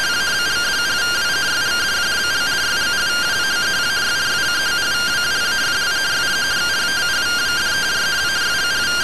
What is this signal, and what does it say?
Two-tone teleprinter signal, about 170 Hz shift.
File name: findings / MR AXIAL 33,99mm 1.gen-20 rtty rx